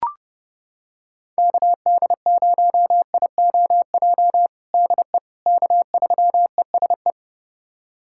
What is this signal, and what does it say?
Morse code, heard as a keyed tone, about 29 wpm, 700 Hz. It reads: KD0IOJ DE K3ESE
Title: USA Callsign DE Callsign 30wpm